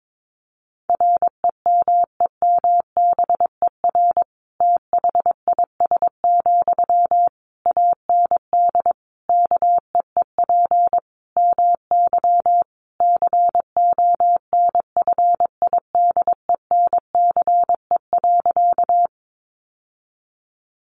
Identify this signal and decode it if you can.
Morse code, heard as a keyed tone, about 22 wpm, 700 Hz. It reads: REMEMBER THIS, AND KEEP MY CONFIDENCE.
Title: Sentences from Top 1500 Words 22wpm